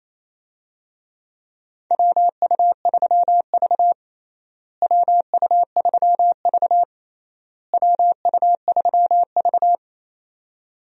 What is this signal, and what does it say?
Morse code, heard as a keyed tone, about 28 wpm, 700 Hz. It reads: WU3V WU3V WU3V
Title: USA Call Signs - 4 Characters - Repeated 3x - 28wpm